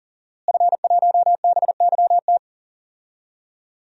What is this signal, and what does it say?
Morse code, heard as a keyed tone, about 40 wpm, 700 Hz. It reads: F1BYT
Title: DXCC-100 Call Signs - 5 Characters - Repeated 3x - 40wpm